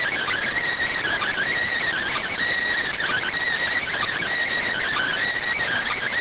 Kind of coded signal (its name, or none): Emergency Alert System (SAME) tones